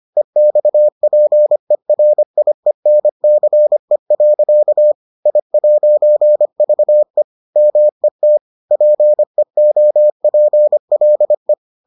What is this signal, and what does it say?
Morse code, heard as a keyed tone, about 25 wpm, 600 Hz. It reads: EXPERIENCE. I'VE MET PEOPLE